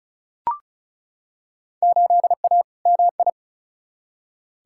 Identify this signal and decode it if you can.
Morse code, heard as a keyed tone, about 36 wpm, 700 Hz. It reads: 8A MI
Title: Field Day Exchanges Encore - 35wpm